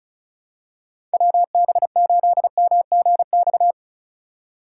Morse code, heard as a keyed tone, about 34 wpm, 700 Hz. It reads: WB8MGX
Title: USA Call Signs - 6 Characters - 2x Character Spacing - 35wpm